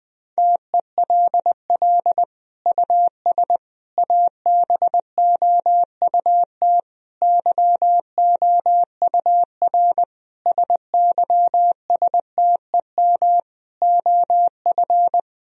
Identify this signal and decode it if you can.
Morse code, heard as a keyed tone, about 21 wpm, 700 Hz. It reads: TELL US ABOUT YOUR SYSTEM OF
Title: Sentences from Top 1000 Words - Review Part 3 - 20wpm